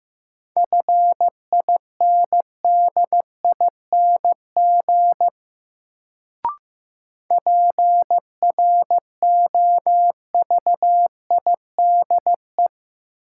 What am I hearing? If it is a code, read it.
Morse code, heard as a keyed tone, about 15 wpm, 700 Hz. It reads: FINDING PROVIDE
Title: Sentences from Top 1100 Words - New Word Drill Alt - 15wpm